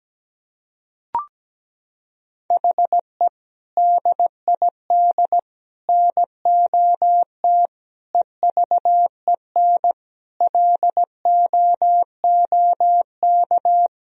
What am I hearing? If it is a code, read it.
Morse code, heard as a keyed tone, about 17 wpm, 700 Hz. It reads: HE DID NOT EVEN LOOK
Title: Sentences from Top 1000 Words - Review Part 1 - 17wpm